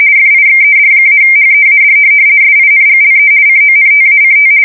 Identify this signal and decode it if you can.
RTTY, about 50 baud. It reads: FIXED-HEAD SINGLE CONTACT
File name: fsk170-50B / teletype-corp